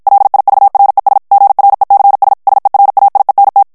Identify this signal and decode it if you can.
Morse code, heard as a keyed tone, about 54 wpm, 800 Hz. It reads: LIVRES GRECS SERAIENT
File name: FRANCAIS TEXTE CLAIR 2 VITESSE 50 MOTS MINUTES